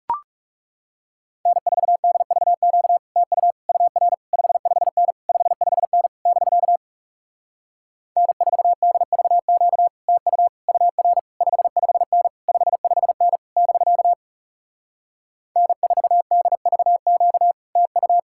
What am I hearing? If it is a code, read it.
Morse code, heard as a keyed tone, about 40 wpm, 700 Hz. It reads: N4DVQ TU UR 55N 55N <BK> N4DVQ TU UR 55N 55N <BK> N4DVQ TU